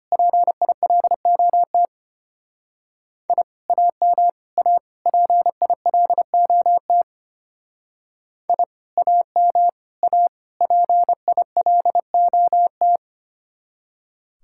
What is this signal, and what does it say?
Morse code, heard as a keyed tone, about 31 wpm, 700 Hz. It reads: PILOT I AM A PILOT I AM A PILOT
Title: Speed-Racing Common QSO Phrases Encore - 25wpm